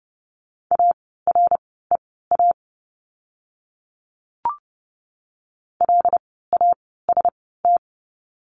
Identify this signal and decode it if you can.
Morse code, heard as a keyed tone, about 30 wpm, spaced as for 17 wpm, 700 Hz. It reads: AREA LAST
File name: course / 30:10wpm / 108 Lesson 18f - Letter m - Words 30wpm